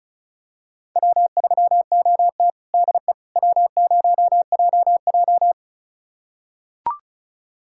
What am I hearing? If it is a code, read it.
Morse code, heard as a keyed tone, about 36 wpm, 700 Hz. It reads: W3OT DE W0JJ
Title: USA Callsign DE Callsign - 4 Characters - 35wpm